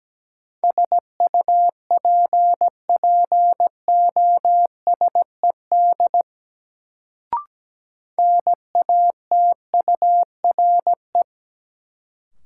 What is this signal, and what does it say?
Morse code, heard as a keyed tone, about 17 wpm, 700 Hz. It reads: SUPPOSED NATURE
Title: Sets of 3 Words - New Word Drill - 17wpm